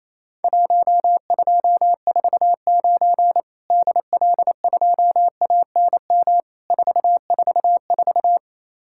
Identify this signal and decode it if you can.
Morse code, heard as a keyed tone, about 27 wpm, 700 Hz. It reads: 1249 DL2ANM 444
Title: Worked All Europe DX Contest - QTC Exchange - 28wpm